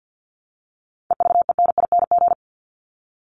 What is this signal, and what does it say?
Morse code, heard as a keyed tone, about 50 wpm, 700 Hz. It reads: EVENING
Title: Top 1501-1750 Words 50wpm